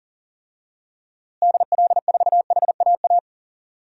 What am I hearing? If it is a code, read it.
Morse code, heard as a keyed tone, about 37 wpm, 700 Hz. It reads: DL4HAA
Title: DXCC-100 Call Signs - 6 Characters - Repeated 3x - 40wpm